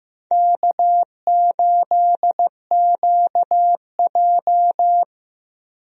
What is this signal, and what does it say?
Morse code, heard as a keyed tone, about 15 wpm, 700 Hz. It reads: K8QJ
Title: USA Call Signs - 4 Characters - Repeated 3x - 15wpm